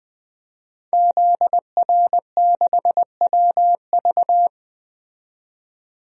Morse code, heard as a keyed tone, about 21 wpm, 700 Hz. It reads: ZR6WV
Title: DXCC-100 Call Signs - 5 Characters - 20wpm